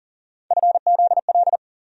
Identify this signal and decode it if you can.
Morse code, heard as a keyed tone, about 40 wpm, 700 Hz. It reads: FZL
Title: USA Call Sign Postfixes - 3 Letters 40wpm